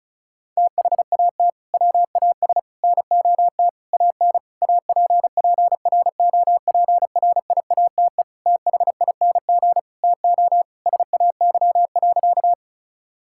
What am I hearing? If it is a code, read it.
Morse code, heard as a keyed tone, about 36 wpm, 690 Hz. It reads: THAT WAS NOT AN APPROPRIATE THING TO SAY.
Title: Sentences from Top 1000 Words - Review Part 5 - 35wpm